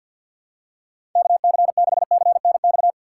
Morse code, heard as a keyed tone, about 46 wpm, 700 Hz. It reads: KC6CNX
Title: USA Call Signs - 6 Characters - 6x Character Spacing - 50wpm